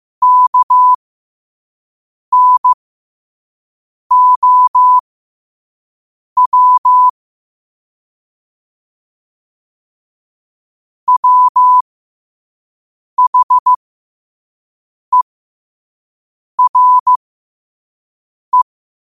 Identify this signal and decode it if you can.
Morse code, heard as a keyed tone, about 15 wpm, spaced as for 5 wpm, 1000 Hz. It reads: KNOW WHERE